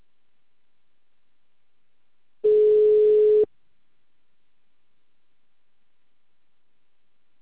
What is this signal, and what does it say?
Ringback tone (Europe, 425 Hz).
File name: out- -2044- - - .48264